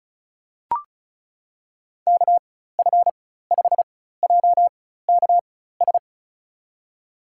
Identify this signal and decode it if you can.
Morse code, heard as a keyed tone, about 34 wpm, spaced as for 16 wpm, 700 Hz. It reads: KF5JKS
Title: USA Call Signs - 4x Character Spacing - 35wpm